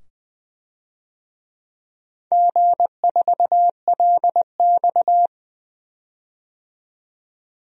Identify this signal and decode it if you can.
Morse code, heard as a keyed tone, about 20 wpm, 700 Hz. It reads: G4LX
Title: DXCC-100 Call Signs - 4 Characters - 20wpm